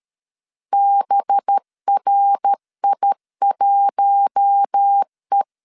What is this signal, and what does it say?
Morse code, heard as a keyed tone, about 13 wpm, 800 Hz. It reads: BRI1E
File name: LEZIONI / 6 PKX0 / 6 PK 12wpm 2006